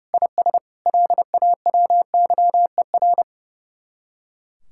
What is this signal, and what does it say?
Morse code, heard as a keyed tone, about 35 wpm, 700 Hz. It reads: IS LAWYER
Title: Speed-Racing Common QSO Phrases - 30wpm